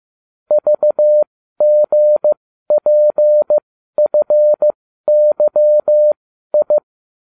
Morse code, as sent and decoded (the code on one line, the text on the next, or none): ...- --. .--. ..-. -.-- ..
VGPFYI